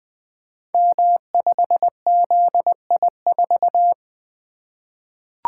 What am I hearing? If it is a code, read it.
Morse code, heard as a keyed tone, about 20 wpm, 700 Hz. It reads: M5ZI4
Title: Code Groups - 5 Letter:Numbers - 20wpm